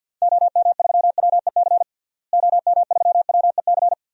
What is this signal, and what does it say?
Morse code, heard as a keyed tone, about 52 wpm, 700 Hz. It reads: OM3WEC OM3WEC
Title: DXCC-100 Call Signs - 6 Characters - Repeated 2x - 50wpm